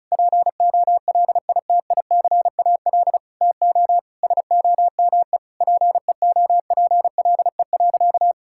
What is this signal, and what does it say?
Morse code, heard as a keyed tone, about 34 wpm, 700 Hz. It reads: POLITICAL TO SOME PEOPLE.
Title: Sentences from Top 1000 Words - Review Part 5 - 35wpm